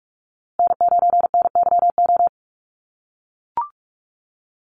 Morse code, heard as a keyed tone, about 44 wpm, 700 Hz. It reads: N9NYO
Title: USA Call Signs Repeated 2x 45wpm